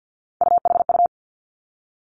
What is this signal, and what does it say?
Morse code, heard as a keyed tone, about 50 wpm, 700 Hz. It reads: UHU
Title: Three Challenging Letter-Number Combinations 50wpm